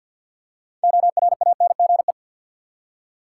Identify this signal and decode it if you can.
Morse code, heard as a keyed tone, about 46 wpm, 700 Hz. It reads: ORANGE
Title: Six Letter Words - 50wpm